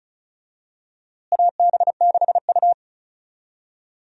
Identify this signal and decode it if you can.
Morse code, heard as a keyed tone, about 34 wpm, 700 Hz. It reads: AB6U
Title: USA Call Signs - 4 Characters - Repeated 3x - 35wpm